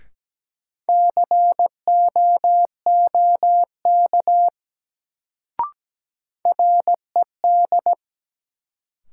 Morse code, heard as a keyed tone, about 17 wpm, 700 Hz. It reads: COOK RED
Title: Sentences from Top 700 Words - New Word Drill - 17wpm